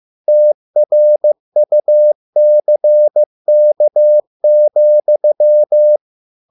Morse code, heard as a keyed tone, about 15 wpm, 600 Hz. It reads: TRUCK,